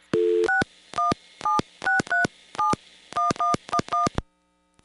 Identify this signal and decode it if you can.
DTMF, heard as telephone touch tones, tones of about 150 ms. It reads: 6176371111